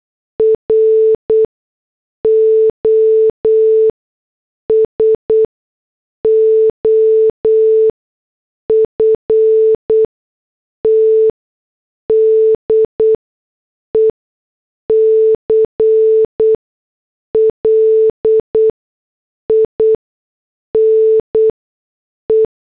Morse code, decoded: ROSOFT DECLINE